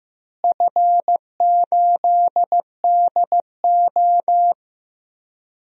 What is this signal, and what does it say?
Morse code, heard as a keyed tone, about 15 wpm, 700 Hz. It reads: F8DO
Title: DXCC-100 Call Signs - 4 Characters - Repeated 3x - 15wpm